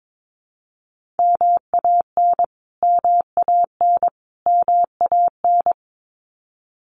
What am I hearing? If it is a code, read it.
Morse code, heard as a keyed tone, about 22 wpm, 700 Hz. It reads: MAN MAN MAN
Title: Top 500 Words Repeated 22wpm